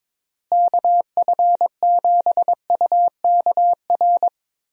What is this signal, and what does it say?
Morse code, heard as a keyed tone, about 22 wpm, 700 Hz. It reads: KF7UKR
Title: USA Call Signs - 6 Characters - Repeated 2x - 22wpm